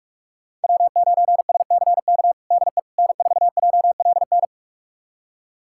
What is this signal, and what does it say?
Morse code, heard as a keyed tone, about 46 wpm, 700 Hz. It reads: W9SCK DE N4JLN
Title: USA Callsign DE Callsign - 5 Characters - 45wpm